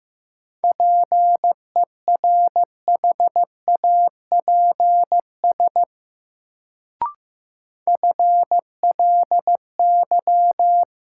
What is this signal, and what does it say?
Morse code, heard as a keyed tone, about 15 wpm, 700 Hz. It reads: PERHAPS FLY
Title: Sentences from Top 500 Words - New Word Drill - 15wpm